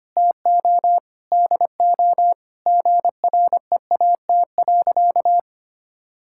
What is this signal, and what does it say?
Morse code, heard as a keyed tone, about 25 wpm, 700 Hz. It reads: TO DO GREAT.
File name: practice / 25wpm / Sentences with 4-Word Phrases - 25wpm